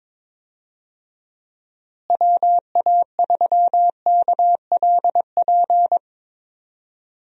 Morse code, heard as a keyed tone, about 22 wpm, 700 Hz. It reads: WA3KLP
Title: USA Call Signs - 6 Characters - Repeated 2x - 22wpm